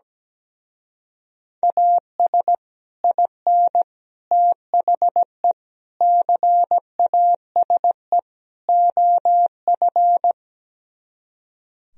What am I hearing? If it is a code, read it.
Morse code, heard as a keyed tone, about 17 wpm, 700 Hz. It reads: AS IN THE CASE OF
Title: Speed-Racing - Sets of 5 Words 17wpm